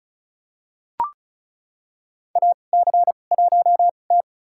Morse code, heard as a keyed tone, about 36 wpm, spaced as for 25 wpm, 700 Hz. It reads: AC1T
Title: USA Call Signs - 2x Character Spacing - 35wpm